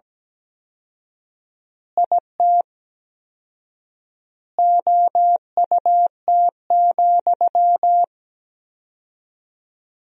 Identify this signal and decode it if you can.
Morse code, heard as a keyed tone, about 17 wpm, 700 Hz. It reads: IT OUT,
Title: 4x Word-Spacing - Sentences from Top 300 Words - 17wpm